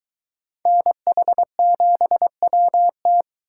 Morse code, heard as a keyed tone, about 22 wpm, 700 Hz. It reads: NH7WT